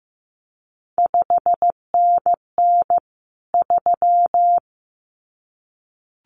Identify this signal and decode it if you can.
Morse code, heard as a keyed tone, about 15 wpm, 700 Hz. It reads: 5NN 3
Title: CQ WW DX Contest Exchange - 15wpm